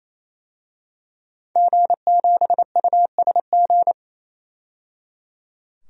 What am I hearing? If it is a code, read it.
Morse code, heard as a keyed tone, about 29 wpm, 690 Hz. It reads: G7USG
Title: DXCC-100 Call Signs - 5 Characters - 28wpm